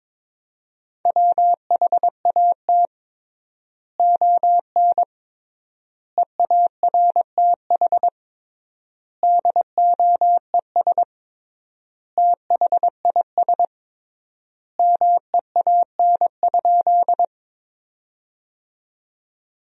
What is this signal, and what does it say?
Morse code, heard as a keyed tone, about 22 wpm, 700 Hz. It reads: WHAT ON EARTH DOES THIS MEAN?
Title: 3x Word-Spacing - Sentences from Top 200 Words - 22wpm